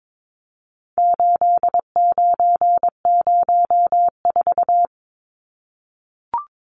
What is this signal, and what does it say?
Morse code, heard as a keyed tone, about 22 wpm, 700 Hz. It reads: 8904
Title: Numbers - 4-Digit 22wpm